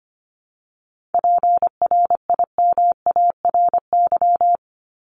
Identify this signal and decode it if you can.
Morse code, heard as a keyed tone, about 25 wpm, 700 Hz. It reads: PRIMARY
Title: LY Suffix - Top 100 Words with Base Words 25wpm